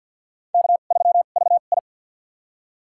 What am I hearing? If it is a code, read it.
Morse code, heard as a keyed tone, about 49 wpm, spaced as for 36 wpm, 700 Hz. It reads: K3VI
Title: USA Call Signs - 4 Characters - 2x Character Spacing - 50wpm